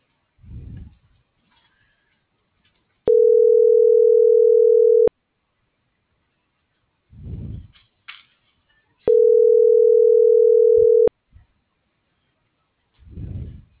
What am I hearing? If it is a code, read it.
Ringback tone (North America), 440 + 480 Hz.